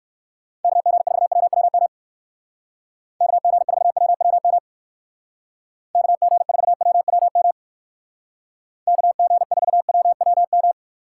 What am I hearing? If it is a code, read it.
Morse code, heard as a keyed tone, about 56 wpm, 700 Hz. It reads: KG4WWM KG4WWM KG4WWM KG4WWM
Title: Speed Racing - USA 2x3 Call Signs 45wpm